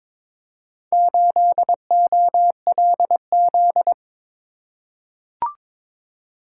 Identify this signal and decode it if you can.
Morse code, heard as a keyed tone, about 22 wpm, 700 Hz. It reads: 8OLZ
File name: practice / 22wpm / Four Letter-Number Combinations - 22wpm